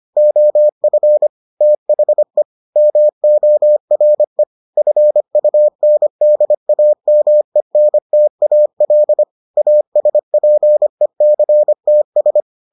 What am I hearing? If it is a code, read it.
Morse code, heard as a keyed tone, about 25 wpm, 600 Hz. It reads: OF THE MORE FUNDAMENTAL ASPECTS